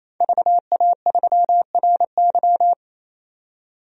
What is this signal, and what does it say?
Morse code, heard as a keyed tone, about 27 wpm, 700 Hz. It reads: VA3RY
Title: DXCC-100 Call Signs - 5 Characters - 28wpm